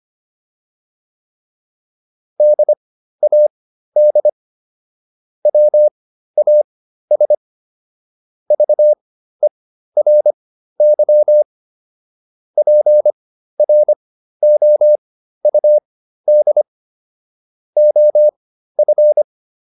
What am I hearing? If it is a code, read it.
Morse code, heard as a keyed tone, about 25 wpm, spaced as for 13 wpm, 600 Hz. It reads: DAD WAS VERY PROUD OF